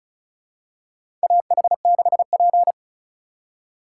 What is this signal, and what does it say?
Morse code, heard as a keyed tone, about 36 wpm, 700 Hz. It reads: AH6P
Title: USA Call Signs - 2x Character Spacing - 35wpm